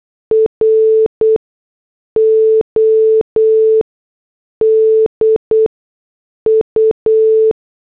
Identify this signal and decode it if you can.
Morse code, heard as a keyed tone, about 8 wpm, 440 Hz. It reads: R O D U
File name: morse / headlines